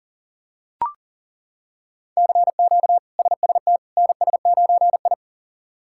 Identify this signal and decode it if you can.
Morse code, heard as a keyed tone, about 37 wpm, 700 Hz. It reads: CQ SST NS9I
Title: K1USN SST QSO Practice - 40wpm